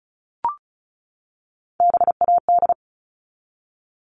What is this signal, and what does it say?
Morse code, heard as a keyed tone, about 34 wpm, 700 Hz. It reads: BAD
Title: Top 501-750 Words 35wpm